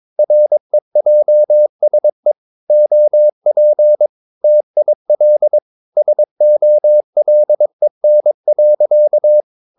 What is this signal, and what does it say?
Morse code, heard as a keyed tone, about 22 wpm, 600 Hz. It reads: REJSE OP TIL SOLEN.
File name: Ferie QRS 21wpm